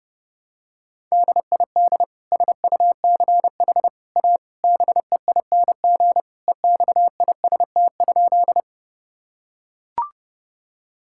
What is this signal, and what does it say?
Morse code, heard as a keyed tone, about 32 wpm, 700 Hz. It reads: DID SUCH A BEING EXIST?
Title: Sentences from Top 1000 Words - Review Part 5 - 30wpm